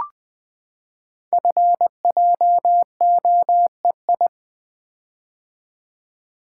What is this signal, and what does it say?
Morse code, heard as a keyed tone, about 20 wpm, 700 Hz. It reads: FJOEI
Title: Code Groups - 5 Letters - 20wpm